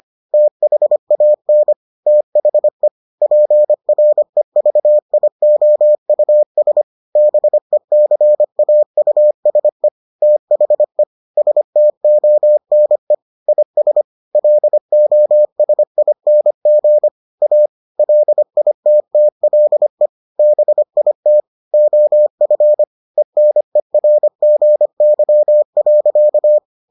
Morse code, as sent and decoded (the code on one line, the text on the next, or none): - .... .- -. / - .... / .--. .-. ...- .. --- ..- ... / -... -.-. .- ..- ... / - .... / ... - --- -. / .. ... / .-.. --- ... .. -. --. / .- / .-.. .. - - .-.. / -... .. - / --- ..-. / . -. .-. --. -.-- .-.-.-
THAN THE PREVIOUS BECAUSE THE STONE IS LOSING A LITTLE BIT OF ENERGY.